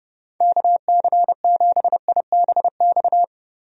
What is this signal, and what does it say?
Morse code, heard as a keyed tone, about 30 wpm, 700 Hz. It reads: KC7IBX
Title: USA Call Signs - 6 Characters - Repeated 3x - 30wpm